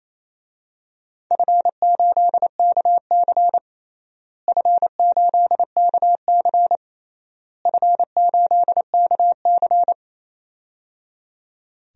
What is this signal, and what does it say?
Morse code, heard as a keyed tone, about 28 wpm, 700 Hz. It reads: F8KC F8KC F8KC
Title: DXCC-100 Call Signs - 4 Characters - Repeated 3x - 28wpm